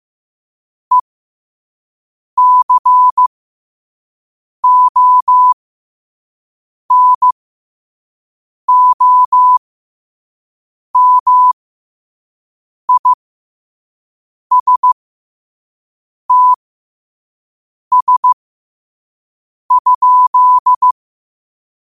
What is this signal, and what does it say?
Morse code, heard as a keyed tone, about 15 wpm, spaced as for 5 wpm, 1000 Hz. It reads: ECONOMISTS?
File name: qotdfiles / 5wpm